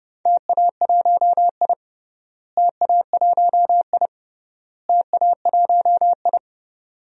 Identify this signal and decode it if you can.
Morse code, heard as a keyed tone, about 31 wpm, 700 Hz. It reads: TA1I TA1I TA1I
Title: DXCC-100 Call Signs - 4 Characters - Repeated 3x - 30wpm